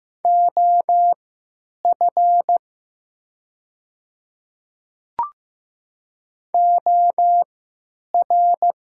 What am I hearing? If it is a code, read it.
Morse code, heard as a keyed tone, about 15 wpm, spaced as for 9 wpm, 700 Hz. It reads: OF OR
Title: 3x Character Spacing - Top 500 Words - 15wpm